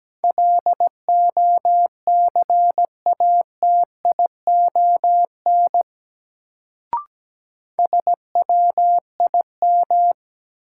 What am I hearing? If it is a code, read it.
Morse code, heard as a keyed tone, about 17 wpm, 700 Hz. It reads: LOCATION SWIM
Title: Sentences from Top 1000 Words - Review Part 6 - New Word Drill - 17wpm